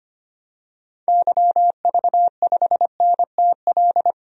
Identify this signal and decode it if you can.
Morse code, heard as a keyed tone, about 25 wpm, 700 Hz. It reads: YV5NTL
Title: DXCC-100 Call Signs - 6 Characters - 25wpm